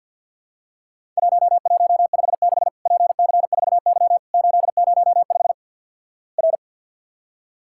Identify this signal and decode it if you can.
Morse code, heard as a keyed tone, about 50 wpm, 700 Hz. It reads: 1156 PC4Y 805 R
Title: Worked All Europe DX Contest - QTC Exchange - 50wpm